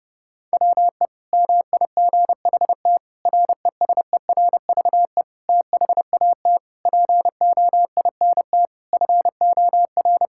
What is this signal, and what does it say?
Morse code, heard as a keyed tone, about 32 wpm, 700 Hz. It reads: WE MIGHT RESERVE THAT POINT FOR
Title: Sentences from Top 1000 Words - Review Part 6 - 30wpm